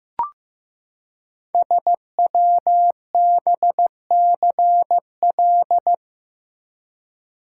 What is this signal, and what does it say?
Morse code, heard as a keyed tone, about 15 wpm, 700 Hz. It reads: SWBCL
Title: Code Groups - 5 Letters - 15wpm